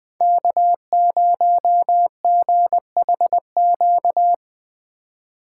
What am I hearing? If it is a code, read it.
Morse code, heard as a keyed tone, about 20 wpm, 700 Hz. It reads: K0GHQ
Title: USA Call Signs - 5 Characters - 2x Character Spacing - 20wpm